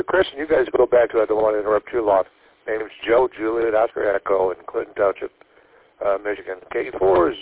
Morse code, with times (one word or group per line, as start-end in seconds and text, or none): none